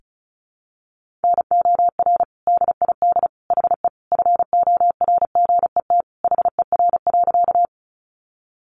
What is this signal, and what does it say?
Morse code, heard as a keyed tone, about 34 wpm, 700 Hz. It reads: NOR DID HE FORGET HER.
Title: Sentences from Top 1500 Words 35wpm